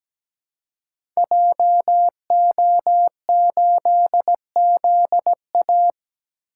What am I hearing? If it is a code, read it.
Morse code, heard as a keyed tone, about 17 wpm, 700 Hz. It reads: JO8ZA
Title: Code Groups - 5 Letter:Numbers - 17wpm